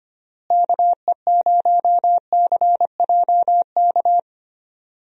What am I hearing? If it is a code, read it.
Morse code, heard as a keyed tone, about 25 wpm, 700 Hz. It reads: KE0CJK
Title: USA Call Signs - 6 Characters - Repeated 2x - 25wpm